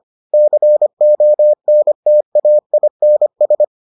Morse code, heard as a keyed tone, about 25 wpm, 600 Hz. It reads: CONTAINS